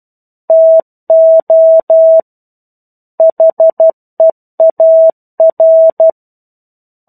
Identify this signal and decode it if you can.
Morse code, heard as a keyed tone, about 12 wpm, 650 Hz. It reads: TO HEAR
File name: 01-Chapter-01 12 WPM